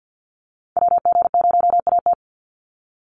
Morse code, heard as a keyed tone, about 50 wpm, 700 Hz. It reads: WG0AT
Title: NA SOTA Chaser Callsigns 50wpm